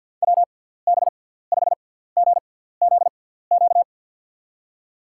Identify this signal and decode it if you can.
Morse code, heard as a keyed tone, about 53 wpm, spaced as for 17 wpm, 700 Hz. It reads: WB5GZQ
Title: USA Call Signs - 6 Characters - 6x Character Spacing - 50wpm